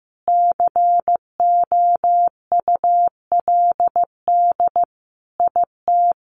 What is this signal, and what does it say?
Morse code, heard as a keyed tone, about 14 wpm, 700 Hz. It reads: COULD IT
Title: Sentences from Top 1200 Words 15wpm